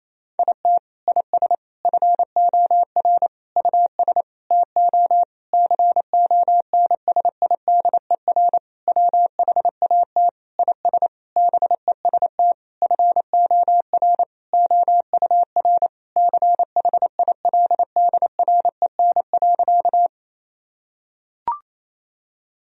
Morse code, heard as a keyed tone, about 28 wpm, 700 Hz. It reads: IT IS FOR US TO CONSIDER WHAT IS BEST FOR OUR CHILDREN.
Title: Sentences from Top 1000 Words - Review Part 4 - 28wpm